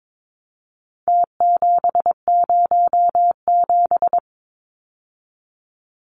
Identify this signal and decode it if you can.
Morse code, heard as a keyed tone, about 22 wpm, 700 Hz. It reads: T707
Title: Numbers - Contest Style - 4-Digit 22wpm